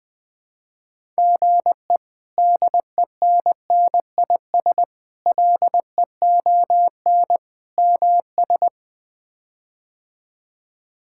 Morse code, heard as a keyed tone, about 20 wpm, 700 Hz. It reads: GE DENNIS LEON MS
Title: K1USN SST QSO Practice - 20wpm